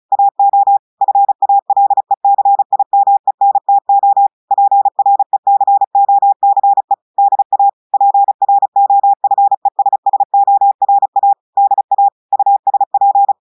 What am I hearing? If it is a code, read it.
Morse code, heard as a keyed tone, about 36 wpm, 800 Hz. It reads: AO FALECIMENTO PRECOCE DA PROFESSORA DA USP
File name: audios / portugais / portugais texte en clair 1 vitesse 35 mots minutes